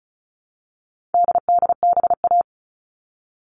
Morse code, heard as a keyed tone, about 34 wpm, 700 Hz. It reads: DDBA
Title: All Cut Numbers - 4-Digit 35wpm